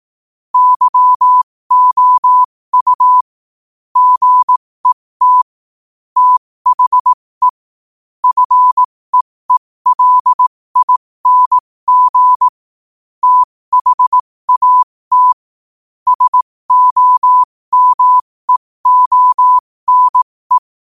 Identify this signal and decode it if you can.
Morse code, heard as a keyed tone, about 18 wpm, spaced as for 15 wpm, 1000 Hz. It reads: YOU GET THE FEELING THAT SOMEONE
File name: qotdfiles / 15wpm